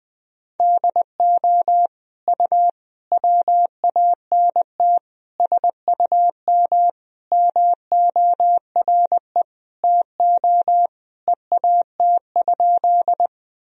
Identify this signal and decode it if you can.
Morse code, heard as a keyed tone, about 20 wpm, 700 Hz. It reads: DO U WANT SUM MORE TO EAT?
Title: Sentences from Top 300 Words and Abbreviations 20wpm